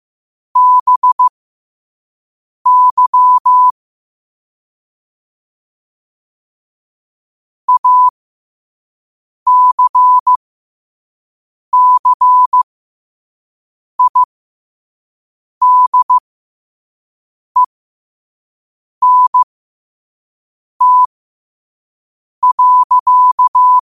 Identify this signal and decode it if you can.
Morse code, heard as a keyed tone, about 15 wpm, spaced as for 5 wpm, 1000 Hz. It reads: BY ACCIDENT.
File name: qotdfiles / 5wpm